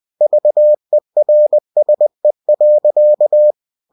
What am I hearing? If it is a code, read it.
Morse code, heard as a keyed tone, about 20 wpm, 600 Hz. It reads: VERSE.
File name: cw-books / winnie-the-pooh / winnie-20wpm-page-0045